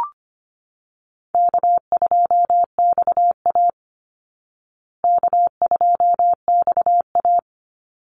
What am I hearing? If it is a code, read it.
Morse code, heard as a keyed tone, about 25 wpm, 700 Hz. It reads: K2XA K2XA
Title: Easy Call Signs Repeated 3x - 25wpm